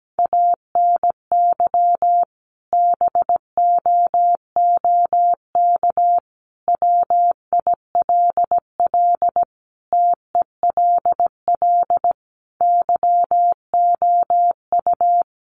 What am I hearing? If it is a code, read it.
Morse code, heard as a keyed tone, about 17 wpm, 700 Hz. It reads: ANY BOOK WILL TELL YOU
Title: Sentences from Top 300 Words 17wpm